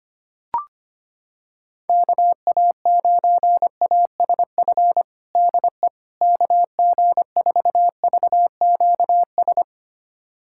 Morse code, heard as a keyed tone, about 25 wpm, 700 Hz. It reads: KA9ASF DE KG4VQS
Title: USA Callsign DE Callsign 25wpm